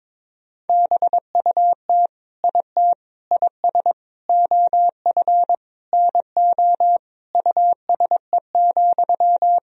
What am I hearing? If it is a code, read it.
Morse code, heard as a keyed tone, about 22 wpm, 700 Hz. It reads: BUT IT IS OF NO USE,
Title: Sentences from Top 1000 Words - Review Part 2 - 22wpm